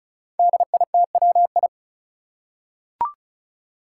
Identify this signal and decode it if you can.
Morse code, heard as a keyed tone, about 36 wpm, 700 Hz. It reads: DITWI
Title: Five Challenging Letter-Number Combinations 35wpm